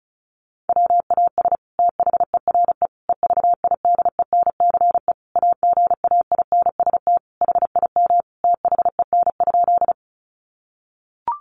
Morse code, heard as a keyed tone, about 34 wpm, 700 Hz. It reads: WAS THERE EVIDENCE AGAINST HIM THEN?
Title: Sentences from Top 1300 Words 35wpm